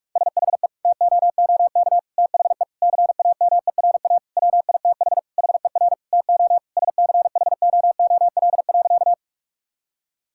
Morse code, heard as a keyed tone, about 46 wpm, 690 Hz. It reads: SHE TOOK THE CAMERA WITH HER TO SCHOOL.
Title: Sentences from Top 1000 Words - Review Part 5 - 45wpm